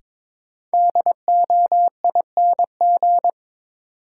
Morse code, heard as a keyed tone, about 23 wpm, 700 Hz. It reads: DOING
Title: Sentences from Top 1400 Words - New Word Drill Alt - 22wpm